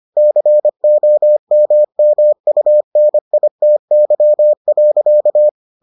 Morse code, heard as a keyed tone, about 25 wpm, 600 Hz. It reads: COMMUNITY.